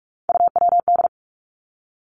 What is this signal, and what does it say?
Morse code, heard as a keyed tone, about 46 wpm, 700 Hz. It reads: UWD
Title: Three Challenging Letter-Number Combinations 45wpm